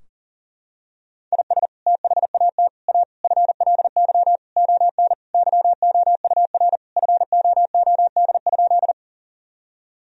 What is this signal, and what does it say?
Morse code, heard as a keyed tone, about 40 wpm, 700 Hz. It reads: IS THAT A FLY ON YOUR FOOD?
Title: Sentences from Top 1000 Words - Review Part 2 - 40wpm